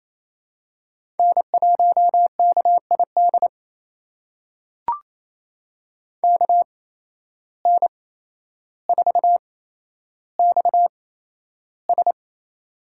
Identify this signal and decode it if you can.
Morse code, heard as a keyed tone, about 28 wpm, spaced as for 12 wpm, 700 Hz. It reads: N1KID KN4XS
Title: USA Call Signs - 5 Characters - 8x Character Spacing - 28wpm